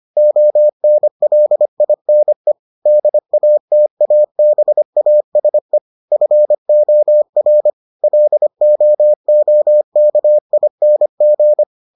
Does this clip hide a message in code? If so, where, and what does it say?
Yes: Morse code from 0.2 to 11.6 s, heard as a keyed tone, about 25 wpm, 600 Hz. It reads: ONLINE DATABASE FOR LOOKING